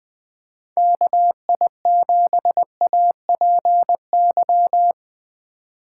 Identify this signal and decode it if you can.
Morse code, heard as a keyed tone, about 21 wpm, 700 Hz. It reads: KI7APY
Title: USA Call Signs - 2x Character Spacing - 20wpm